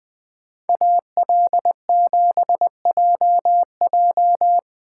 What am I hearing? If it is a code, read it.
Morse code, heard as a keyed tone, about 20 wpm, 700 Hz. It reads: AL7JJ
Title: Speed Racing - USA 2x2 Call Signs 17wpm